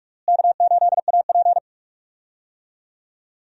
Morse code, heard as a keyed tone, about 46 wpm, 700 Hz. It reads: K8AP
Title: USA Call Signs - 4 Characters - Repeated 2x - 45wpm